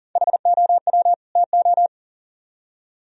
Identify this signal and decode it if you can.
Morse code, heard as a keyed tone, about 50 wpm, 700 Hz. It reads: HOW TO
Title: Speed-Racing - Sets of 2 Words 40wpm